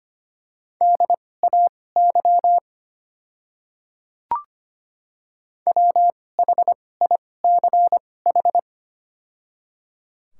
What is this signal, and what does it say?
Morse code, heard as a keyed tone, about 25 wpm, spaced as for 18 wpm, 700 Hz. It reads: DAY WHICH
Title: 2x Character Spacing - Top 100 Words - 25wpm